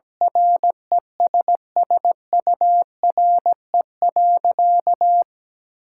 Morse code, heard as a keyed tone, about 17 wpm, 700 Hz. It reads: RESSURE.
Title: Sentences from Top 1000 Words - Review Part 6 - 17wpm